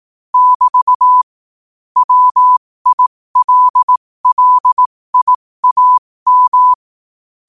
Morse code, heard as a keyed tone, about 18 wpm, spaced as for 15 wpm, 1000 Hz. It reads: <BT> WILLIAM
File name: qotdfiles / 15wpm